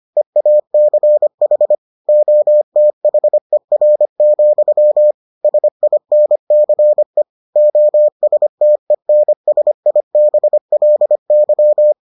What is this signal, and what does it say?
Morse code, heard as a keyed tone, about 25 wpm, 600 Hz. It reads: EACH OTHER, SINCE OSTENSIBLY